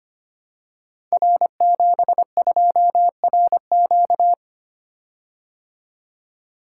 Morse code, heard as a keyed tone, about 25 wpm, 700 Hz. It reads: R72RQ
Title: Code Groups - 5 Letter:Numbers - 25wpm